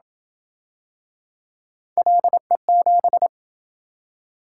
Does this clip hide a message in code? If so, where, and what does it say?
Yes: Morse code from 2.0 to 3.3 s, heard as a keyed tone, about 27 wpm, 700 Hz. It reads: LE7